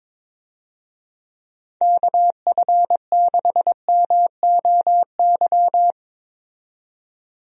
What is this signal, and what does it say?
Morse code, heard as a keyed tone, about 22 wpm, 700 Hz. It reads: KF6MOY
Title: USA Call Signs - 6 Characters - 22wpm